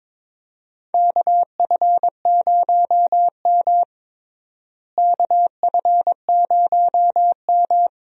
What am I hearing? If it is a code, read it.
Morse code, heard as a keyed tone, about 22 wpm, 700 Hz. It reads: KF0M KF0M
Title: USA Call Signs - 4 Characters - Repeated 2x - 22wpm